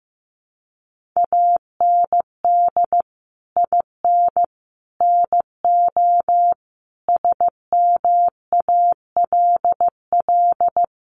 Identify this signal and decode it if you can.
Morse code, heard as a keyed tone, about 15 wpm, 700 Hz. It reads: AND IN NO SMALL
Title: Sentences from Top 1500 Words 15wpm